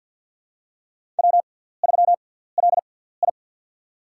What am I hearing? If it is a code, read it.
Morse code, heard as a keyed tone, about 48 wpm, spaced as for 17 wpm, 700 Hz. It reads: W3LI
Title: USA Call Signs - 6x Character Spacing - 50wpm